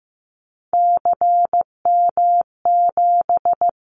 Morse code, heard as a keyed tone, about 15 wpm, 700 Hz. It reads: CM7
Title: Callsign Prefixes for Top 100 DXCC - 15wpm